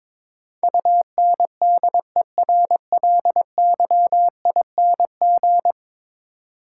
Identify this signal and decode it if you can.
Morse code, heard as a keyed tone, about 22 wpm, 700 Hz. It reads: UNDERLYING
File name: practice / 22wpm / 10 Letter Words - First 3 Letters Given - 22wpm